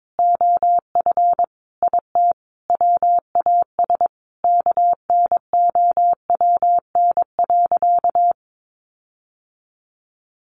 Morse code, heard as a keyed tone, about 22 wpm, 700 Hz. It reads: OF IT WAS KNOWN.
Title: Sentences from Top 1400 Words 22wpm